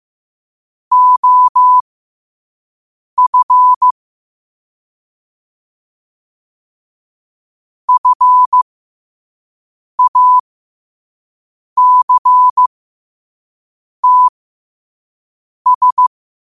Morse code, heard as a keyed tone, about 15 wpm, spaced as for 5 wpm, 1000 Hz. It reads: OF FACTS
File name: qotdfiles / 5wpm